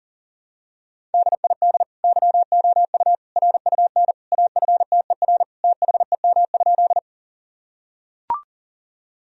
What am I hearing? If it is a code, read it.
Morse code, heard as a keyed tone, about 40 wpm, 700 Hz. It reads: DID YOU RUN AFTER THEM?
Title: Sentences from Top 1000 Words - Review Part 1 - 40wpm